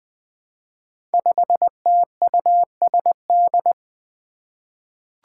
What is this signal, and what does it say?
Morse code, heard as a keyed tone, about 21 wpm, 700 Hz. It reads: 5TUSD
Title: Five Challenging Letter-Number Combinations 20wpm